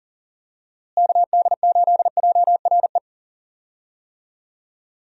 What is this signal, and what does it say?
Morse code, heard as a keyed tone, about 37 wpm, 700 Hz. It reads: KD8JRE
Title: USA Call Signs - 6 Characters - Repeated 3x - 40wpm